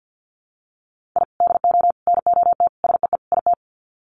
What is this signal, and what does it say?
Morse code, heard as a keyed tone, about 50 wpm, 700 Hz. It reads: I DO NOT SEE IT
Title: Sets of 5 Words Encore2 50wpm